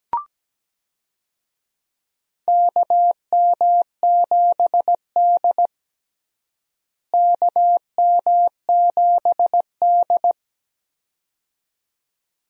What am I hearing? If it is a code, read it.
Morse code, heard as a keyed tone, about 17 wpm, 700 Hz. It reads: KM7D KM7D
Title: USA Call Signs - 4 Characters - Repeated 2x - 17wpm